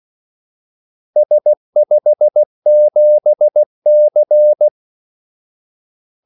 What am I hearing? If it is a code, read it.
Morse code, heard as a keyed tone, about 16 wpm, 600 Hz. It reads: S57C
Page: Kaldesignaler 16 wpm